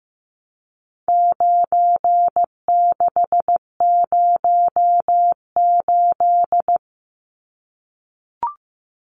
Morse code, heard as a keyed tone, about 15 wpm, 700 Hz. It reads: 9608
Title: Numbers - 4-Digit 15wpm